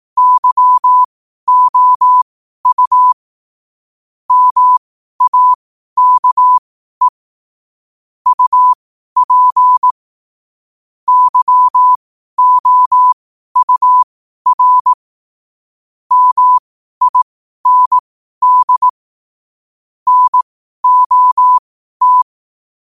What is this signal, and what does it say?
Morse code, heard as a keyed tone, about 18 wpm, spaced as for 12 wpm, 1000 Hz. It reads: YOU MAKE UP YOUR MIND NOT
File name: qotdfiles / 12wpm